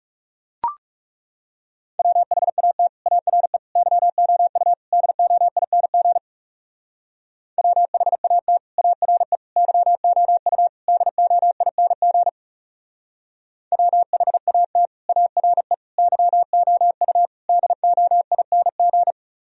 Morse code, heard as a keyed tone, about 35 wpm, 700 Hz. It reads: WHAT ARE YOU DOING WHAT ARE YOU DOING WHAT ARE YOU DOING
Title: Speed-Racing - Sets of 4 Words 30wpm